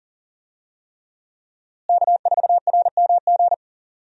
Morse code, heard as a keyed tone, about 40 wpm, 700 Hz. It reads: K4RMG
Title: USA Call Signs - 5 Characters - Repeated 3x - 40wpm